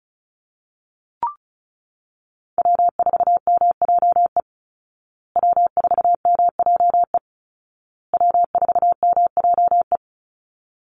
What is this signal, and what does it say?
Morse code, heard as a keyed tone, about 34 wpm, 700 Hz. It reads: W4MJE W4MJE W4MJE
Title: Call Signs Repeated 2x - 35wpm